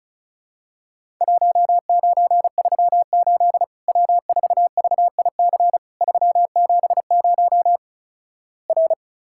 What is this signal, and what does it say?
Morse code, heard as a keyed tone, about 36 wpm, 700 Hz. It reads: 1938 W4VIC 370 R
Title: Worked All Europe DX Contest - QTC Exchange - 35wpm